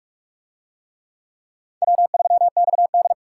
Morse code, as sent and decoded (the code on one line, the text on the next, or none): .-- ...-- -..- -..
W3XD